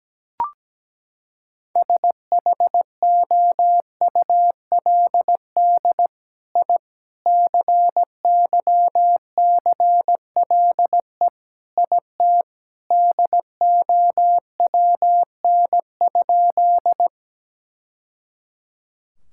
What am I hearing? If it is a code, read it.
Morse code, heard as a keyed tone, about 17 wpm, 700 Hz. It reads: SHOULD I CYCLE IT DOWN?
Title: Sentences from Top 1000 Words - Review Part 6 - 17wpm